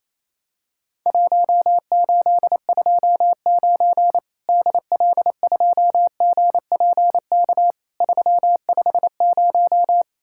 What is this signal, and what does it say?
Morse code, heard as a keyed tone, about 28 wpm, 700 Hz. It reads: 1829 DL2GPK 350
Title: Worked All Europe DX Contest - QTC Exchange - 28wpm